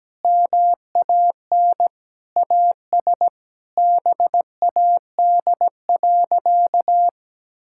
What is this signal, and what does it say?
Morse code, heard as a keyed tone, about 17 wpm, 700 Hz. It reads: MAN AS BAD.
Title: Sentences from Top 1000 Words - Review Part 6 - 17wpm